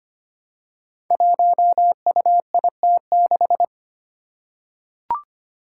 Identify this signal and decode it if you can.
Morse code, heard as a keyed tone, about 25 wpm, 700 Hz. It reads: 1UIT6
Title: Five Challenging Letter-Number Combinations 25wpm